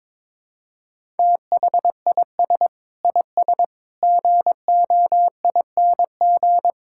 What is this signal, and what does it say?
Morse code, heard as a keyed tone, about 22 wpm, 700 Hz. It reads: THIS IS GOING
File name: practice / 22wpm / Build A Phrase - 5-Word Phrases - 22wpm